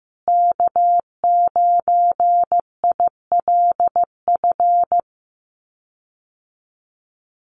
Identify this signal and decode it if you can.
Morse code, heard as a keyed tone, about 15 wpm, 700 Hz. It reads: K9ILF
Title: USA Call Signs Repeated 3x 15wpm